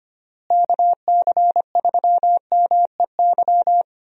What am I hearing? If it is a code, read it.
Morse code, heard as a keyed tone, about 25 wpm, 700 Hz. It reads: KC3MEY
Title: USA Call Signs - 6 Characters - Repeated 3x - 25wpm